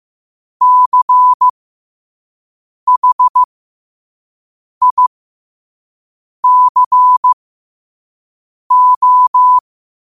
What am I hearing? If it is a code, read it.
Morse code, heard as a keyed tone, about 15 wpm, spaced as for 5 wpm, 1000 Hz. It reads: CHICO
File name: qotdfiles / 5wpm